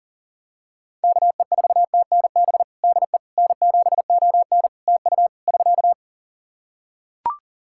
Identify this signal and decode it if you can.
Morse code, heard as a keyed tone, about 43 wpm, 700 Hz. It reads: KE4TNB DE N7ON TU <SK>
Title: Practice QSO - Exchange 4 - Ending - 40wpm